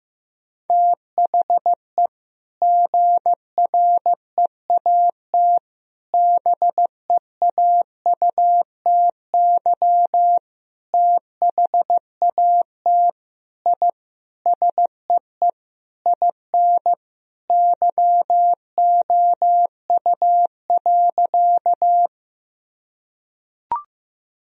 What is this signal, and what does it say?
Morse code, heard as a keyed tone, about 15 wpm, 700 Hz. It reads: THE GREAT BEAUTY THAT I SEE IN YOU.
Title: Sentences from Top 1000 Words - Review Part 2 - 15wpm